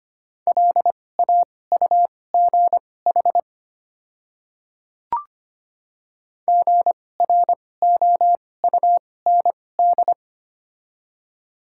Morse code, heard as a keyed tone, about 25 wpm, spaced as for 18 wpm, 700 Hz. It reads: LAUGH GROUND
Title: 2x Character Spacing - Top 500 Words - 25wpm